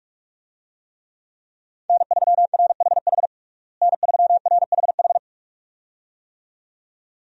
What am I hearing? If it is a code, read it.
Morse code, heard as a keyed tone, about 41 wpm, 700 Hz. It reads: N3RHH N3RHH
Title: USA Call Signs - 5 Characters - Repeated 2x - 45wpm